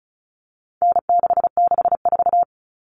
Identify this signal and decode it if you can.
Morse code, heard as a keyed tone, about 34 wpm, 700 Hz. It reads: N664
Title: All Cut Numbers - 4-Digit 35wpm